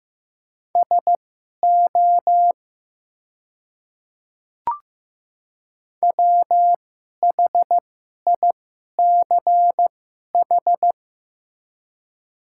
Morse code, heard as a keyed tone, about 15 wpm, spaced as for 11 wpm, 700 Hz. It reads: SO WHICH
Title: 2x Character Spacing - Top 100 Words - 15wpm